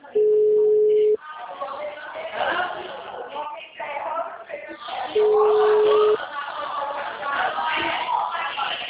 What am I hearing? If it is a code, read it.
Ringback tone (Europe, 425 Hz), with speech.